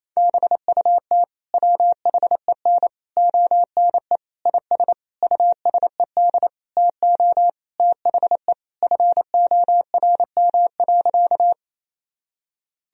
Morse code, heard as a keyed tone, about 28 wpm, 700 Hz. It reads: BUT WHEN ONE IS USED TO THE FORM.
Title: Sentences from Top 1000 Words - Review Part 1 - 28wpm